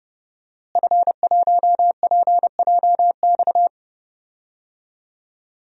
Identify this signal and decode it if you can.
Morse code, heard as a keyed tone, about 31 wpm, 700 Hz. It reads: F1PJX
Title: DXCC-100 Call Signs - 5 Characters - Repeated 3x - 30wpm